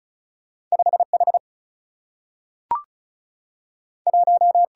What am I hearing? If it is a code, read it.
Morse code, heard as a keyed tone, about 36 wpm, 700 Hz. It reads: 5H 1
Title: Single-Letter Single-Number 35wpm